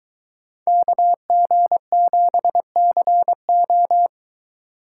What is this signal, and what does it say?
Morse code, heard as a keyed tone, about 23 wpm, 700 Hz. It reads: KG7CO